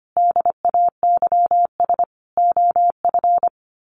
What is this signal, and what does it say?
Morse code, heard as a keyed tone, about 25 wpm, 700 Hz. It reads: DAYS OF
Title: Sets of 5 Words Encore 25wpm